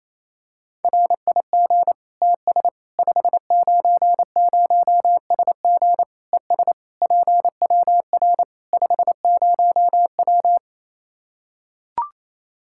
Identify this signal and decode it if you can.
Morse code, heard as a keyed tone, about 28 wpm, 700 Hz. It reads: RIG TS 590SG ES PWR 50W
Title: K1USN SST QSO Practice 28wpm